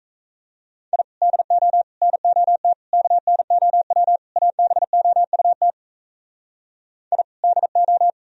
Morse code, heard as a keyed tone, about 37 wpm, 700 Hz. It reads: I DO NOT KNOW ABOUT I DO